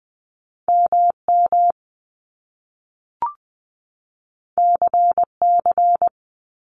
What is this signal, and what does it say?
Morse code, heard as a keyed tone, about 20 wpm, 700 Hz. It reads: MM CC
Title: Top Two Letter Combinations in Words 20wpm